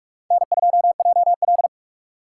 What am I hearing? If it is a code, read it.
Morse code, heard as a keyed tone, about 44 wpm, 700 Hz. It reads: N2JL